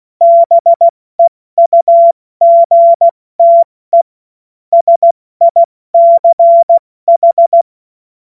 Morse code, heard as a keyed tone, about 16 wpm, 680 Hz. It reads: BEUGTE SICH
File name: Morsekurs / Witze / Witz 09 Januar 2024 70 F80